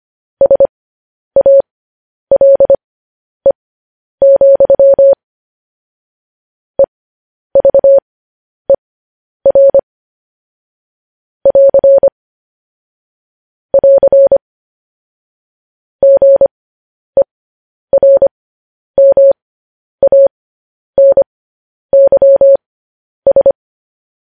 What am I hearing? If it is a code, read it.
Morse code, heard as a keyed tone, about 25 wpm, spaced as for 10 wpm, 550 Hz. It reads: SALE, EVER <AR> <AR> GERMANYS
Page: news-headlines-in-morse-code-at-10-wpm